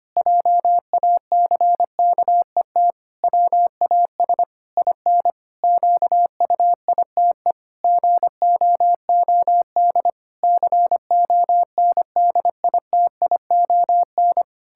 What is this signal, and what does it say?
Morse code, heard as a keyed tone, about 25 wpm, 700 Hz. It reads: JACKET WAS IN QUITE GOOD CONDITION
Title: Pangram Challenge v2 - 25wpm